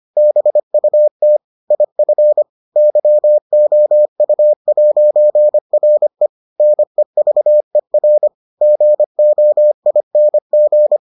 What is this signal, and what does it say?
Morse code, heard as a keyed tone, about 25 wpm, 600 Hz. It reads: BUT IF YOU'RE NEVER GOING